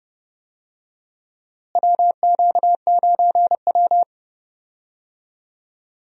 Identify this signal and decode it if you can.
Morse code, heard as a keyed tone, about 30 wpm, 700 Hz. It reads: WQ9W
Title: USA Call Signs - 4 Characters - Repeated 2x - 30wpm